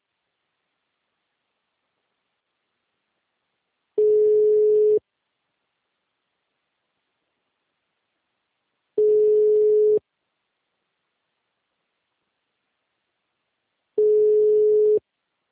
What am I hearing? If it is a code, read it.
Ringback tone (Europe, 425 Hz).
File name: out- -2027- - - .32285